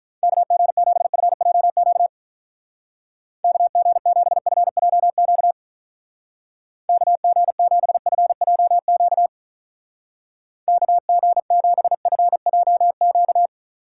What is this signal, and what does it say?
Morse code, heard as a keyed tone, about 43 wpm, 700 Hz. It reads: KG7FJQ KG7FJQ KG7FJQ KG7FJQ
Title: Speed Racing - USA Call Signs 35wpm